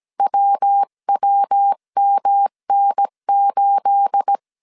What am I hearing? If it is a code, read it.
Morse code, heard as a keyed tone, about 17 wpm, 800 Hz. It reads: WWMN8
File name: CW / DET / 1 9 1500C S B 16wpm 2856